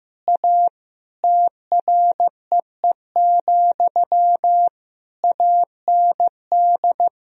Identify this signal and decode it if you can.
Morse code, heard as a keyed tone, about 15 wpm, 700 Hz. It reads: A TREE, AND
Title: Sentences from Top 1000 Words - Review Part 2 - 15wpm